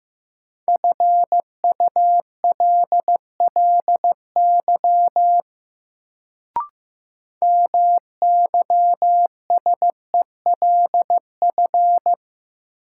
Morse code, heard as a keyed tone, about 15 wpm, 700 Hz. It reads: FULLY MYSELF
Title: Sentences from Top 1300 Words - New Word Drill - 15wpm